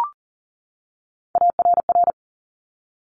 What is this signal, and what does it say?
Morse code, heard as a keyed tone, about 41 wpm, 700 Hz. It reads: ARR
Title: Top Three Letter Combinations in Words 40wpm